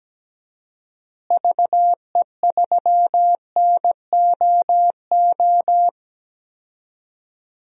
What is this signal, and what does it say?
Morse code, heard as a keyed tone, about 17 wpm, 700 Hz. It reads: VE3NOO
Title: DXCC-100 Call Signs - 6 Characters - Repeated 2x - 17wpm